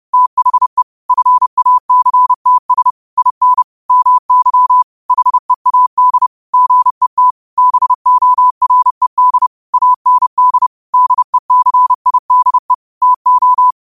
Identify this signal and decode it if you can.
Morse code, heard as a keyed tone, about 30 wpm, 1000 Hz. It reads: THE FACTS IN MY HEAD GET BORED AND DECIDE TO
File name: qotdfiles / 30wpm